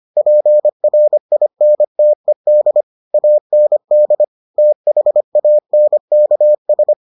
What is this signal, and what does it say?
Morse code, heard as a keyed tone, about 25 wpm, 600 Hz. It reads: PRINTED AND THANKS